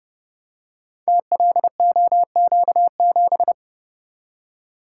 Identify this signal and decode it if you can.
Morse code, heard as a keyed tone, about 30 wpm, 700 Hz. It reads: TLOQ7
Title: Code Groups - 5 Letter:Numbers - 30wpm